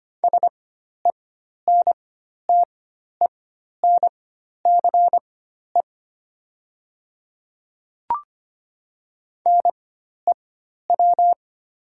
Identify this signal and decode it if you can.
Morse code, heard as a keyed tone, about 25 wpm, spaced as for 12 wpm, 690 Hz. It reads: SENTENCE NEW
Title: 4x Character Spacing - Top 200 Words - 25wpm